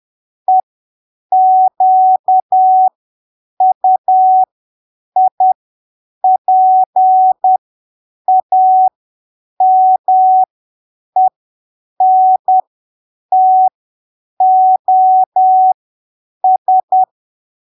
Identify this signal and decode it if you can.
Morse code, heard as a keyed tone, about 10 wpm, spaced as for 7 wpm, 750 Hz. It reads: EQUIPAMENTOS
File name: arquivos / texto / texto 13 10ppm